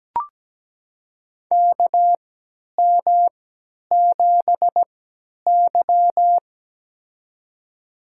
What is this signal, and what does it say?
Morse code, heard as a keyed tone, about 17 wpm, spaced as for 10 wpm, 700 Hz. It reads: KM7Y
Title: USA Call Signs - 4 Characters - 3x Character Spacing - 17wpm